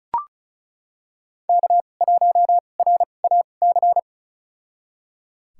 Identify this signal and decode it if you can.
Morse code, heard as a keyed tone, about 36 wpm, spaced as for 25 wpm, 700 Hz. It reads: K1RAC
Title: USA Call Signs - 5 Characters - 2x Character Spacing - 35wpm